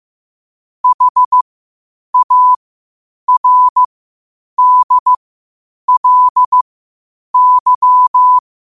Morse code, heard as a keyed tone, about 15 wpm, spaced as for 8 wpm, 1000 Hz. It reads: HARDLY
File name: qotdfiles / 8wpm